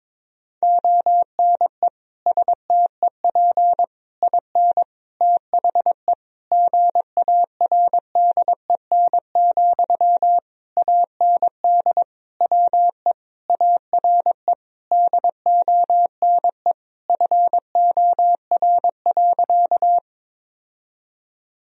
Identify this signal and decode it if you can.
Morse code, heard as a keyed tone, about 22 wpm, 700 Hz. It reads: ONE STEP IN THE GARDEN, AND WE ARE DONE FOR.
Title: Sentences from Top 1000 Words - Review Part 4 - 22wpm